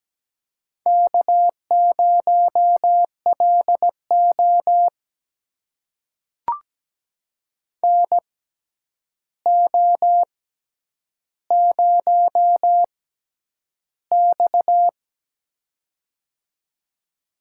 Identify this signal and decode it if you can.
Morse code, heard as a keyed tone, about 17 wpm, spaced as for 9 wpm, 700 Hz. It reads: K0LO NO0X
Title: USA Call Signs - 6x Character Spacing - 17wpm